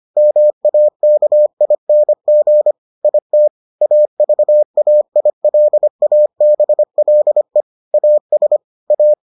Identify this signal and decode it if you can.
Morse code, heard as a keyed tone, about 25 wpm, 600 Hz. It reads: MAKING IT AVAILABLE AS A